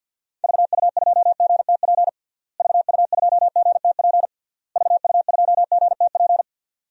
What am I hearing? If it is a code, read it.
Morse code, heard as a keyed tone, about 49 wpm, 700 Hz. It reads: VU2GTP VU2GTP VU2GTP
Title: DXCC-100 Call Signs - 6 Characters - Repeated 3x - 50wpm